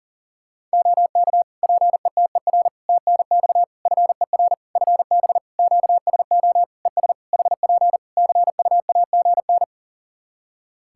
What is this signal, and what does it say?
Morse code, heard as a keyed tone, about 40 wpm, 700 Hz. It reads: OK PETER TNX FER FB QSO ES HP CUAGN
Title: K1USN SST QSO Practice 40wpm